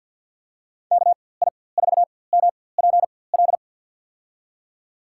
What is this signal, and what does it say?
Morse code, heard as a keyed tone, about 46 wpm, spaced as for 23 wpm, 700 Hz. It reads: KI4MPL
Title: USA Call Signs - 6 Characters - 4x Character Spacing - 50wpm